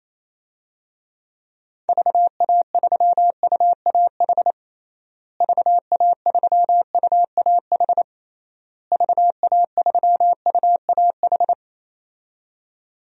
Morse code, heard as a keyed tone, about 28 wpm, 700 Hz. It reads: VA3UAH VA3UAH VA3UAH
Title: DXCC-100 Call Signs - 6 Characters - Repeated 3x - 28wpm